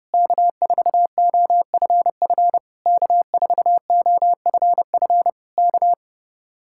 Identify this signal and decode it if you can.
Morse code, heard as a keyed tone, about 29 wpm, 700 Hz. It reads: K4OFF K4OFF K